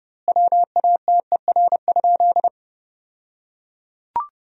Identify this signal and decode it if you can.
Morse code, heard as a keyed tone, about 30 wpm, 700 Hz. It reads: WATER?
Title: 2x Word-Spacing - Sentences from Top 200 Words - 30wpm